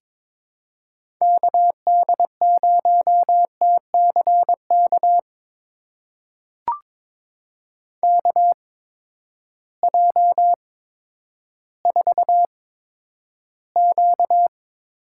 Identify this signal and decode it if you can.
Morse code, heard as a keyed tone, about 22 wpm, spaced as for 11 wpm, 700 Hz. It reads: KD0TCK KJ4Q
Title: USA Call Signs - 6 Characters - 8x Character Spacing - 22wpm